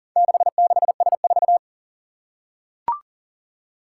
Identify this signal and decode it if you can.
Morse code, heard as a keyed tone, about 40 wpm, 700 Hz. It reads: 66S4
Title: Four Challenging Letter-Number Combinations 40wpm